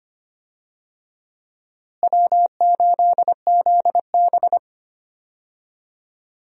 Morse code, heard as a keyed tone, about 25 wpm, 700 Hz. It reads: W8ZB
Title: USA Call Signs - 4 Characters - Repeated 3x - 25wpm